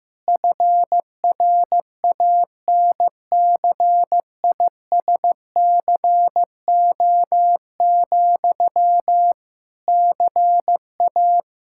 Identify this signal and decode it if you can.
Morse code, heard as a keyed tone, about 15 wpm, 700 Hz. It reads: FRANCISCO, CA
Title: K1USN SST QSO Practice 15wpm